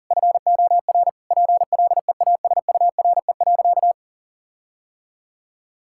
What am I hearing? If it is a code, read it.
Morse code, heard as a keyed tone, about 40 wpm, 700 Hz. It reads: FOR PLEASURE.
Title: Sentences from Top 1600 Words 40wpm